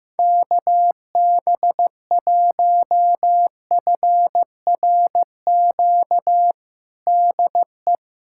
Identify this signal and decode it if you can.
Morse code, heard as a keyed tone, about 15 wpm, 700 Hz. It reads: KB1FRQ DE
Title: USA Callsign DE Callsign - 6 Characters - 15wpm